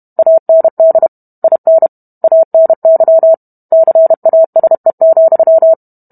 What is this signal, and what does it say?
Morse code, heard as a keyed tone, about 32 wpm, 650 Hz. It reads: AND IN ANY CASE,